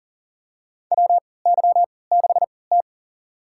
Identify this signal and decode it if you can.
Morse code, heard as a keyed tone, about 40 wpm, spaced as for 23 wpm, 700 Hz. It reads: WY6T
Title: USA Call Signs - 4 Characters - 3x Character Spacing - 40wpm